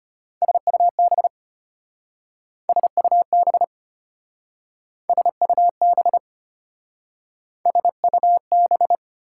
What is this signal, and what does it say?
Morse code, heard as a keyed tone, about 32 wpm, 690 Hz. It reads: SUB SUB SUB SUB